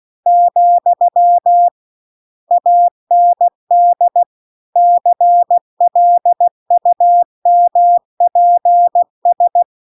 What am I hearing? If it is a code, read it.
Morse code, heard as a keyed tone, about 16 wpm, 700 Hz. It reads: , AND CLUMPS